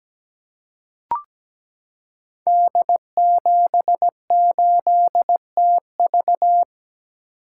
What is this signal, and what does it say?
Morse code, heard as a keyed tone, about 17 wpm, 700 Hz. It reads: D78TV
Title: Code Groups - 5 Letter:Numbers - 17wpm